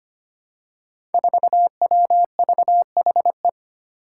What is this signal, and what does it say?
Morse code, heard as a keyed tone, about 25 wpm, 700 Hz. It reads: 4WVHE
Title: Five Challenging Letter-Number Combinations 25wpm